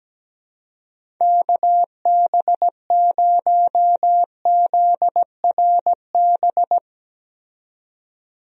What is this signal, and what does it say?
Morse code, heard as a keyed tone, about 17 wpm, 700 Hz. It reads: KB0ZRB
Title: USA Call Signs - 6 Characters - 17wpm